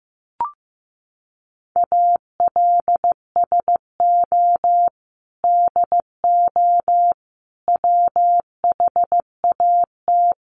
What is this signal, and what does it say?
Morse code, heard as a keyed tone, about 15 wpm, 700 Hz. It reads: ALSO DO WHAT
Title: Sentences from Top 1300 Words 15wpm